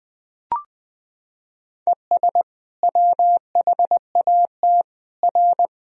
Morse code, heard as a keyed tone, about 21 wpm, 700 Hz. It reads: ES WHAT R
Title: Sentences from Top 200 Words and Abbreviations 20wpm